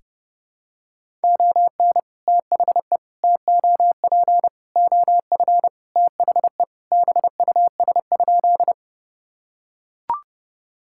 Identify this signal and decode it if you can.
Morse code, heard as a keyed tone, about 30 wpm, 700 Hz. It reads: ON THE TOP OF THE BUS?
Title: Sentences from Top 1000 Words - Review Part 6 - 30wpm